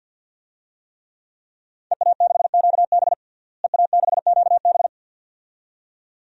Morse code, heard as a keyed tone, about 47 wpm, 700 Hz. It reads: EA6QB EA6QB
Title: DXCC-100 Call Signs - 5 Characters - Repeated 2x - 50wpm